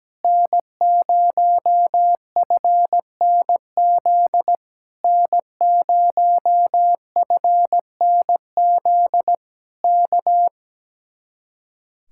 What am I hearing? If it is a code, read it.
Morse code, heard as a keyed tone, about 17 wpm, 700 Hz. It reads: N0FNZ N0FNZ K
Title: Speed-Racing Common QSO Phrases - 17wpm